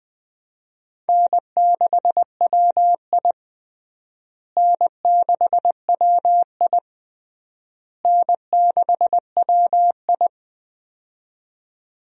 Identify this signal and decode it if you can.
Morse code, heard as a keyed tone, about 20 wpm, 700 Hz. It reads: N6WI N6WI N6WI
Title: USA Call Signs - 4 Characters - Repeated 3x - 20wpm